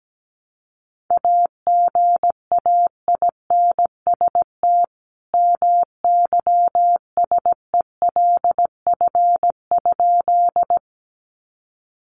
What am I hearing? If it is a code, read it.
Morse code, heard as a keyed tone, about 17 wpm, 700 Hz. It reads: AGAINST MYSELF?
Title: Sentences from Top 1300 Words 17wpm